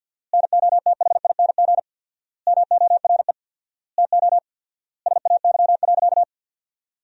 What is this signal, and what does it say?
Morse code, heard as a keyed tone, about 50 wpm, 700 Hz. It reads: NOTHING MORE TO SAY.
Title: 4x Word-Spacing - Sentences from Top 500 Words - 50wpm